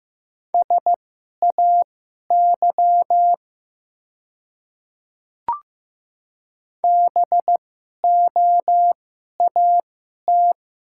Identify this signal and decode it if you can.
Morse code, heard as a keyed tone, about 15 wpm, spaced as for 11 wpm, 700 Hz. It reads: SAY BOAT
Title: 2x Character Spacing - Top 500 Words - 15wpm